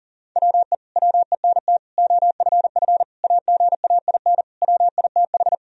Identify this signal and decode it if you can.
Morse code, heard as a keyed tone, about 40 wpm, 700 Hz. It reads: WE WENT OFF AGAIN WITH
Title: Sentences from Top 1000 Words - Review Part 3 - 40wpm